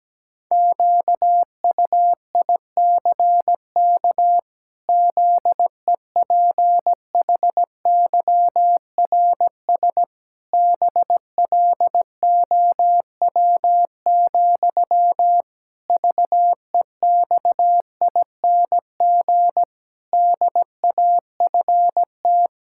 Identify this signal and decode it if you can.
Morse code, heard as a keyed tone, about 17 wpm, 700 Hz. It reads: QUICK ZEPHYRS BLOW, VEXING DAFT
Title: Pangram Challenge v2 - 17wpm